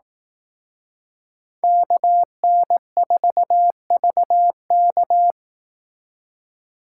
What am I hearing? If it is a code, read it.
Morse code, heard as a keyed tone, about 18 wpm, 700 Hz. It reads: KN4VK
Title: Speed Racing - USA 2x2 Call Signs 15wpm